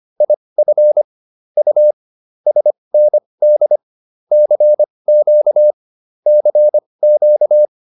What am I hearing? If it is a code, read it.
Morse code, heard as a keyed tone, about 25 wpm, spaced as for 20 wpm, 600 Hz. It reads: IF U SND CQ CQ